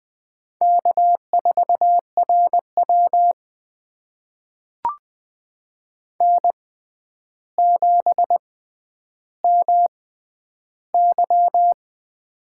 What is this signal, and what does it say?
Morse code, heard as a keyed tone, about 20 wpm, spaced as for 10 wpm, 700 Hz. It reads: K4RW N7MY
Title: USA Call Signs - 6x Character Spacing - 20wpm